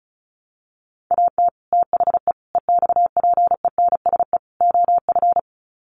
Morse code, heard as a keyed tone, about 34 wpm, 700 Hz. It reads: AT THE EXPENSE OF
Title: Sets of 4 Words 35wpm